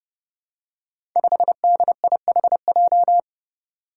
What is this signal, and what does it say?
Morse code, heard as a keyed tone, about 30 wpm, 700 Hz. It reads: 5DIHJ
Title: Five Challenging Letter-Number Combinations 30wpm